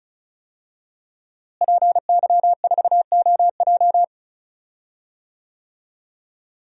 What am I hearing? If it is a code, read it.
Morse code, heard as a keyed tone, about 34 wpm, 700 Hz. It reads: PY4OJ
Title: DXCC-100 Call Signs - 5 Characters - Repeated 3x - 35wpm